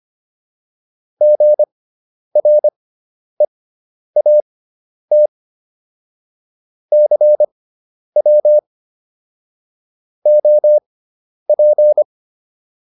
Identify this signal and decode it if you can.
Morse code, heard as a keyed tone, about 25 wpm, spaced as for 10 wpm, 600 Hz. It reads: GREAT CW OP